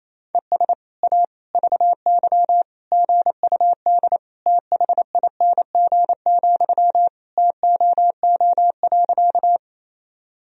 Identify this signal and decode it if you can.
Morse code, heard as a keyed tone, about 27 wpm, 700 Hz. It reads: ES A VY GUD THING, TOO.
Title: Sentences from Top 200 Words and Abbreviations 28wpm